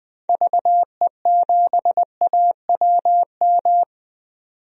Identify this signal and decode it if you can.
Morse code, heard as a keyed tone, about 21 wpm, 700 Hz. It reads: VE7AWM
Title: DXCC-100 Call Signs - 6 Characters - 20wpm